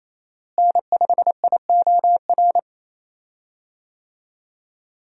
Morse code, heard as a keyed tone, about 27 wpm, 700 Hz. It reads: N5IOR
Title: USA Call Signs - 5 Characters - Repeated 3x - 28wpm